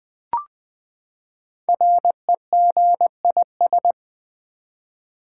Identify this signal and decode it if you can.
Morse code, heard as a keyed tone, about 20 wpm, 700 Hz. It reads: REGIS
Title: Build A Word - 5 Letters At A Time - 12-Letter Words - 20wpm